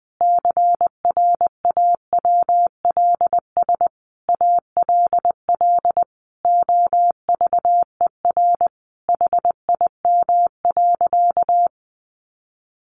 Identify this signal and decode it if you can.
Morse code, heard as a keyed tone, about 20 wpm, 700 Hz. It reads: CRAWLS ALL OVER HIM.
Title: Additional Easy Phrases 20wpm